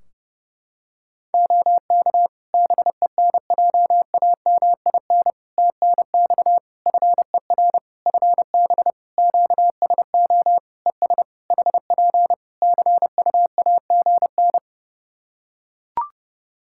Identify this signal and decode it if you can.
Morse code, heard as a keyed tone, about 30 wpm, 700 Hz. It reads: OK BENJAMIN TNX FER FB QSO ES HP CUAGN
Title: K1USN SST QSO Practice 30wpm